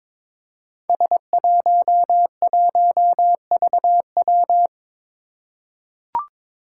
Morse code, heard as a keyed tone, about 23 wpm, 700 Hz. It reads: S11VW
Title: Five Challenging Letter-Number Combinations 22wpm